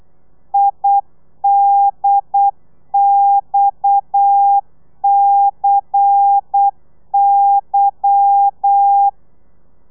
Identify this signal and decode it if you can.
Morse code, heard as a keyed tone, about 8 wpm, 800 Hz. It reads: IDXCY
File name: telegrafia / i2tfj / Lez10 35